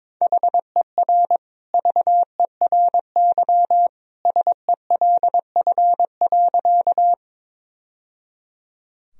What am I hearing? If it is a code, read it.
Morse code, heard as a keyed tone, about 22 wpm, 700 Hz. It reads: HER VERY SELF.
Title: Sentences from Top 1000 Words - Review Part 1 - 22wpm